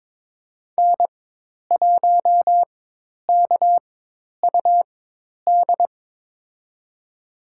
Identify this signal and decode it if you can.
Morse code, heard as a keyed tone, about 22 wpm, spaced as for 10 wpm, 700 Hz. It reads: N1KUD
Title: USA Call Signs - 5 Characters - 4x Character Spacing - 22wpm